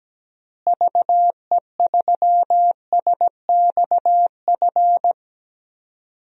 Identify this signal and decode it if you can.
Morse code, heard as a keyed tone, about 17 wpm, 700 Hz. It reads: VE3SXF
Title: DXCC-100 Call Signs - 6 Characters - Repeated 2x - 17wpm